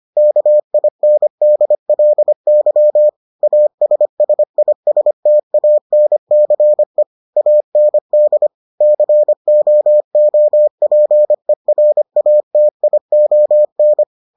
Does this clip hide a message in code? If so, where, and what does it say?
Yes: Morse code from 0.2 to 14.0 s, heard as a keyed tone, about 25 wpm, 600 Hz. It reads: KINDLY ASSISTANCE AND COOPERATION